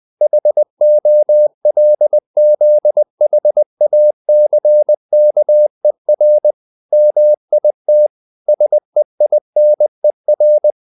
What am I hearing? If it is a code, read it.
Morse code, heard as a keyed tone, about 20 wpm, 600 Hz. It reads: HOLZHACKER MIT SEINER